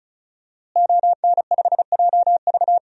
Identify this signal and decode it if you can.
Morse code, heard as a keyed tone, about 34 wpm, 700 Hz. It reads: ON5JV
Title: DXCC-100 Call Signs - 5 Characters - Repeated 2x - 35wpm